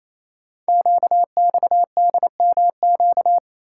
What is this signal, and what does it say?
Morse code, heard as a keyed tone, about 29 wpm, 700 Hz. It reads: QXDMQ
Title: Code Groups - 5 Letters - 28wpm